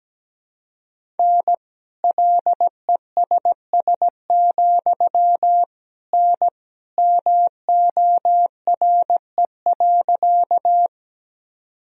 Morse code, heard as a keyed tone, about 17 wpm, 700 Hz. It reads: N LESS, N MORE.
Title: Sentences from Top 400 Words and Abbreviations 17wpm